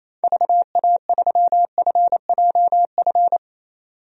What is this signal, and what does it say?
Morse code, heard as a keyed tone, about 28 wpm, 700 Hz. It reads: VA3FJF
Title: DXCC-100 Call Signs - 6 Characters - 28wpm